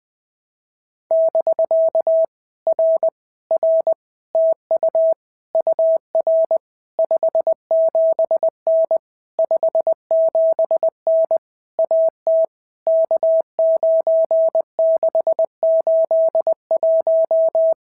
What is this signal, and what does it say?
Morse code, heard as a keyed tone, about 20 wpm, 650 Hz. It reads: <BK> R R TU UR 57N 57N AT K9681
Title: POTA QSO Practice - 20wpm